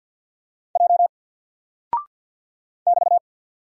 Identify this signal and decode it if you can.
Morse code, heard as a keyed tone, about 48 wpm, 700 Hz. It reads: J <BT>
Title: Single Character (Letter:Number:Punctuation:Prosign) Alt 50wpm